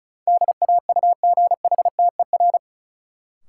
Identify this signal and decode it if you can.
Morse code, heard as a keyed tone, about 36 wpm, 700 Hz. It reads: DAUGHTER
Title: Sentences from Top 1500 Words - New Word Drill - 35wpm